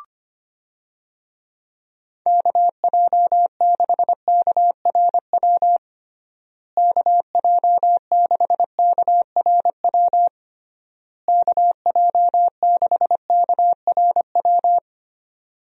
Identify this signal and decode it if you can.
Morse code, heard as a keyed tone, about 25 wpm, 700 Hz. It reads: KJ6KRW KJ6KRW KJ6KRW
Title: USA Call Signs - 6 Characters - Repeated 3x - 25wpm